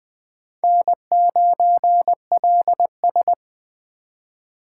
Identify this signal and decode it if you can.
Morse code, heard as a keyed tone, about 20 wpm, 700 Hz. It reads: N9LS
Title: USA Call Signs - 4 Characters - 2x Character Spacing - 20wpm